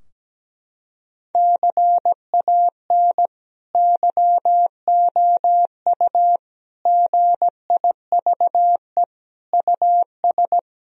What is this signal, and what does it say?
Morse code, heard as a keyed tone, about 17 wpm, 700 Hz. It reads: CAN YOU GIVE US
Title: Sentences from Top 1000 Words - Review Part 3 - 17wpm